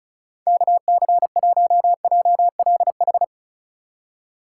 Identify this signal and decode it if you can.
Morse code, heard as a keyed tone, about 36 wpm, 700 Hz. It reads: KC1JLH
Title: USA Call Signs - 2x Character Spacing - 35wpm